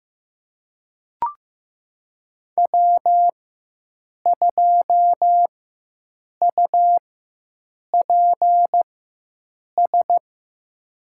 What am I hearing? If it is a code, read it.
Morse code, heard as a keyed tone, about 15 wpm, spaced as for 7 wpm, 700 Hz. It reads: W2UPS
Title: USA Call Signs - 5 Characters - 4x Character Spacing - 15wpm